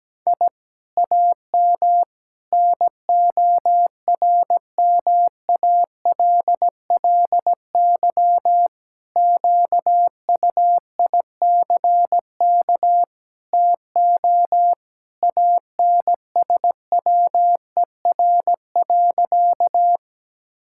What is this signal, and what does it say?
Morse code, heard as a keyed tone, about 17 wpm, 700 Hz. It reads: I AM NORMALLY QUICK TO ANSWER.
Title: Sentences from Top 1000 Words - Review Part 6 - 17wpm